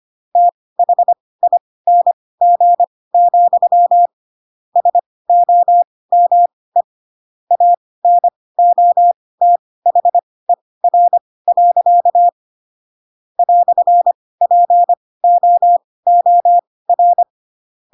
Morse code, heard as a keyed tone, about 25 wpm, spaced as for 18 wpm, 700 Hz. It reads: THING, SOME ANOTHER. "POOR
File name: Pinocchio ch3 12 25 ik1hge